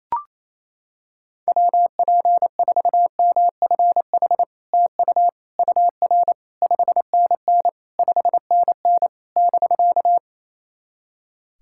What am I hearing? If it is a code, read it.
Morse code, heard as a keyed tone, about 28 wpm, 690 Hz. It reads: WP4MFH TU UR 5NN 5NN <BK>
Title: SOTA QSO Practice 28wpm